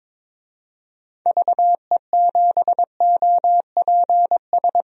Morse code, heard as a keyed tone, about 22 wpm, 700 Hz. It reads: VE7OPS
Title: DXCC-100 Call Signs - 6 Characters - 22wpm